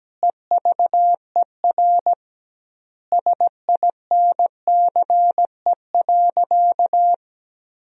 Morse code, heard as a keyed tone, about 17 wpm, 700 Hz. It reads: EVER SINCE.
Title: 2x Word-Spacing - Sentences from Top 400 Words - 17wpm